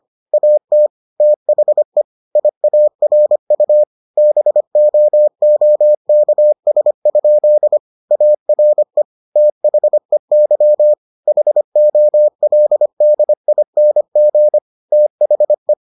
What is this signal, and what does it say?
Morse code, heard as a keyed tone, about 25 wpm, 600 Hz. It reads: AT THE IARU BOOKS? ARE THEY HOLDING THE